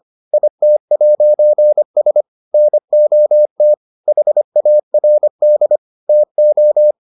Morse code, heard as a keyed tone, about 25 wpm, 600 Hz. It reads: IT'S NOT HARD TO